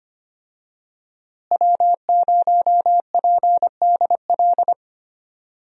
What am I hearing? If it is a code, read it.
Morse code, heard as a keyed tone, about 25 wpm, 700 Hz. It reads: W0PDL
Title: USA Call Signs - 5 Characters - Repeated 3x - 25wpm